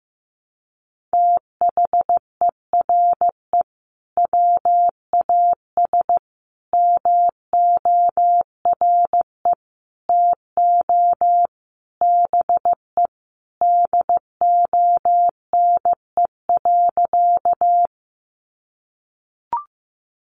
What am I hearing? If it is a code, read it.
Morse code, heard as a keyed tone, about 15 wpm, 700 Hz. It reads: THERE WAS MORE TO BE DONE.
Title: Sentences from Top 500 Words 15wpm